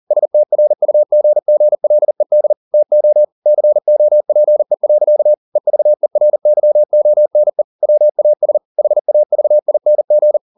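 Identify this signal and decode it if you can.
Morse code, heard as a keyed tone, about 40 wpm, 600 Hz. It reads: STRUGGLED TO COPE. EVERYONE WAS HAVING
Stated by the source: Morse code at 40 wpm, practice text, CW med Gnister.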